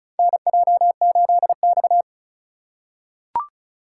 Morse code, heard as a keyed tone, about 34 wpm, 700 Hz. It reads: NJ8X
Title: USA Call Signs - 4 Characters - 4x Character Spacing - 35wpm